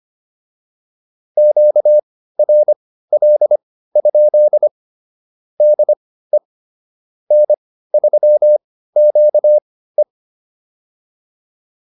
Morse code, heard as a keyed tone, about 25 wpm, spaced as for 15 wpm, 600 Hz. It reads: QRL? DE N3QE